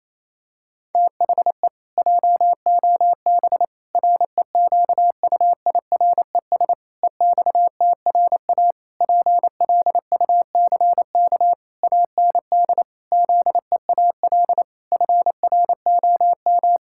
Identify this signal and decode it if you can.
Morse code, heard as a keyed tone, about 28 wpm, 700 Hz. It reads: THE JOB REQUIRES EXTRA PLUCK AND ZEAL FROM
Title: Pangram Challenge v2 - 28wpm